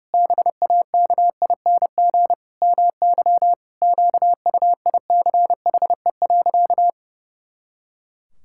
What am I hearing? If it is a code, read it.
Morse code, heard as a keyed tone, about 30 wpm, 700 Hz. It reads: BAKING MY QUICHE.
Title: Pangram Challenge v2 - 30wpm